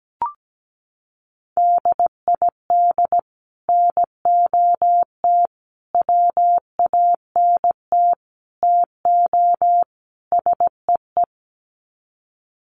Morse code, heard as a keyed tone, about 17 wpm, 700 Hz. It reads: DID NOT WANT TO SEE
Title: Sets of 5 Words Encore2 17wpm